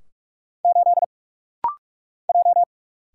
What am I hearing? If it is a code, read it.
Morse code, heard as a keyed tone, about 46 wpm, 700 Hz. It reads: 8 J
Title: Single Letter-Number - Rapid-Fire 45wpm